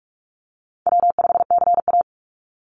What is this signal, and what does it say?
Morse code, heard as a keyed tone, about 46 wpm, 700 Hz. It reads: W5CA
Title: USA Call Signs 45wpm